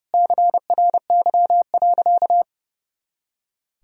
Morse code, heard as a keyed tone, about 30 wpm, 700 Hz. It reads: CRY.
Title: Sentences from Top 500 Words and Abbreviations 30wpm